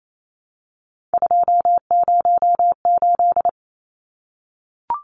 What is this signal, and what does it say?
Morse code, heard as a keyed tone, about 28 wpm, 700 Hz. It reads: 208
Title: Numbers - 3-Digit 28wpm